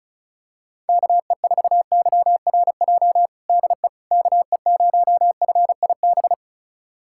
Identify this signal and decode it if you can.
Morse code, heard as a keyed tone, about 34 wpm, 700 Hz. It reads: KE4YRJ DE KE0FIB
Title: USA Callsign DE Callsign - 6 Characters - 35wpm